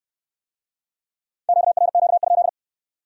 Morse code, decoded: QRQ?

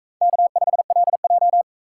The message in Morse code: K5LJ

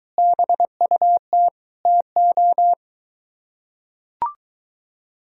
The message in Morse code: BUT TO

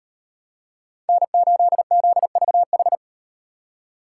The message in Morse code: N8ZVH

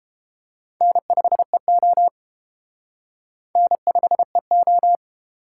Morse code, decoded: N5EO N5EO

